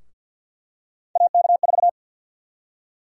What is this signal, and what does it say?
Morse code, heard as a keyed tone, about 47 wpm, 700 Hz. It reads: AK4
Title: USA Call Sign Prefixes - Repeated 2x - 50wpm